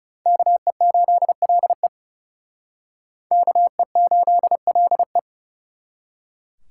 Morse code, heard as a keyed tone, about 36 wpm, 700 Hz. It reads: KE8LE KE8LE
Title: Speed Racing - USA 2x2 Call Signs 30wpm